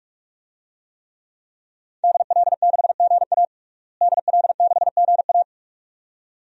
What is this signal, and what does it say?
Morse code, heard as a keyed tone, about 46 wpm, 700 Hz. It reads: DL6GA DL6GA
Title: DXCC-100 Call Signs - 5 Characters - Repeated 2x - 45wpm